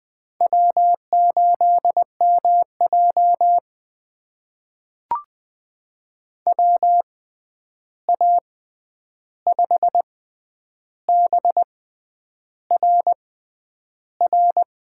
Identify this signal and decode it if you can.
Morse code, heard as a keyed tone, about 20 wpm, spaced as for 9 wpm, 700 Hz. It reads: W8MJ WA5BRR
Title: USA Call Signs - 6x Character Spacing - 20wpm